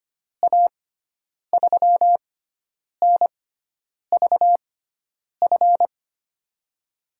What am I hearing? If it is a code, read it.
Morse code, heard as a keyed tone, about 25 wpm, spaced as for 9 wpm, 700 Hz. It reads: A3NVF
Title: USA Call Signs - 6 Characters - 6x Character Spacing - 25wpm